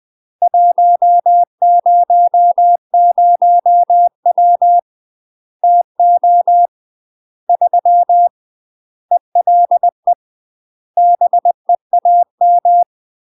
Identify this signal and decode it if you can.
Morse code, heard as a keyed tone, about 20 wpm, 700 Hz. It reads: 100W TO 3 ELE BEAM
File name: UK MASTER QSOs TRAINING / 20WPM / UK MASTER QSOs 0029